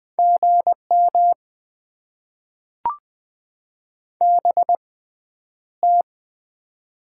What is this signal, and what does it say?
Morse code, heard as a keyed tone, about 20 wpm, 700 Hz. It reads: G M B T
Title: Top 20 QSO Elements - 6x Character Spacing - 20wpm